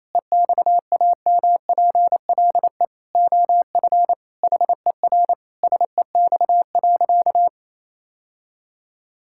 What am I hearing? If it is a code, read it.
Morse code, heard as a keyed tone, about 28 wpm, 700 Hz. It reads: EXAMPLE OF HER SEX.
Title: Sentences from Top 1600 Words 28wpm